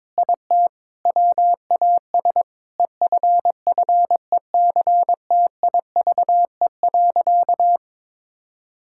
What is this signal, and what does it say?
Morse code, heard as a keyed tone, about 22 wpm, 700 Hz. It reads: IT WAS EFFECTIVE.
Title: Sentences from Top 1000 Words - Review Part 6 - 22wpm